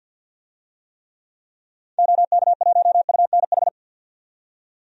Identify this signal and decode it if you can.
Morse code, heard as a keyed tone, about 47 wpm, 700 Hz. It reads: OK1UNH
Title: DXCC-100 Call Signs - 6 Characters - 50wpm